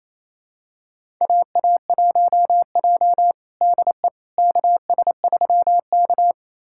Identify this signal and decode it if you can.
Morse code, heard as a keyed tone, about 28 wpm, 700 Hz. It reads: AA1J DE KS3K
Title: USA Callsign DE Callsign - 4 Characters - 28wpm